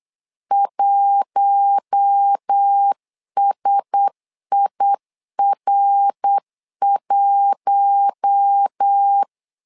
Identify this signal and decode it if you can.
Morse code, heard as a keyed tone, about 8 wpm, 800 Hz. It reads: 1SIR1
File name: LEZIONI / 4 Q1S 8wpm 2710